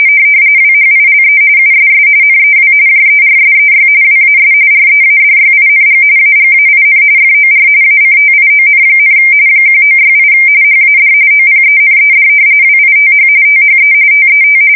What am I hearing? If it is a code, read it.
RTTY, about 45.45 baud. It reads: WHILE USS NEW JERSEY WAS READYING AGAIN FOR ACTION, 1968 HAD OPENED WITH THE TET
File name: fsk170 / battleshipnj